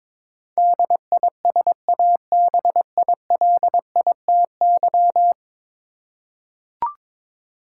Morse code, heard as a keyed tone, about 21 wpm, 700 Hz. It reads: DISABILITY
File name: practice / 22wpm / 10 Letter Words - First 3 Letters Given - 22wpm